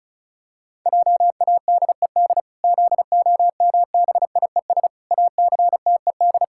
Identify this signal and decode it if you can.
Morse code, heard as a keyed tone, about 34 wpm, 700 Hz. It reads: JADED ZOMBIES ACTED
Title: Pangram Challenge v2 - 35wpm